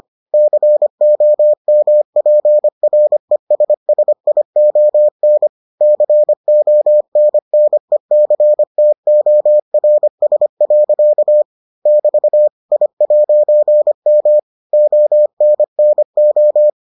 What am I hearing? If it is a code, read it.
Morse code, heard as a keyed tone, about 25 wpm, 600 Hz. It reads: COMPRESSION CONNECTORS. <BT> I'M ONNO